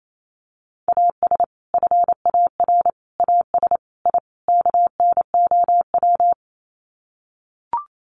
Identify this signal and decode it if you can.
Morse code, heard as a keyed tone, about 28 wpm, 700 Hz. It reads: AS FAR AS I KNOW
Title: Sets of 5 Words 28wpm